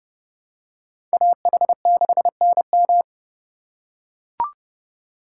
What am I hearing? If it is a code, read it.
Morse code, heard as a keyed tone, about 28 wpm, 700 Hz. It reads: AH6NM